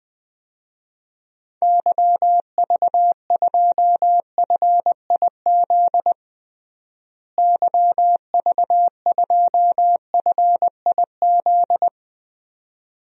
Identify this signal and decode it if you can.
Morse code, heard as a keyed tone, about 21 wpm, 700 Hz. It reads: YV2FIZ YV2FIZ
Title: DXCC-100 Call Signs - 6 Characters - Repeated 2x - 20wpm